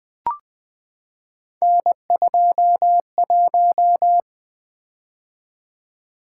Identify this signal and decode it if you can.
Morse code, heard as a keyed tone, about 20 wpm, 700 Hz. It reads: N21
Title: Three Letter-Number Combinations - 20wpm